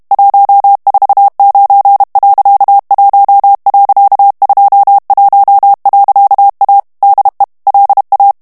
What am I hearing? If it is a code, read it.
Morse code, heard as a keyed tone, about 32 wpm, 800 Hz. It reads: 149.1.21.A DE LA
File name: ESPAGNOL TEXTE EN CLAIR 2 VITESSE 30 MOTS MINUTES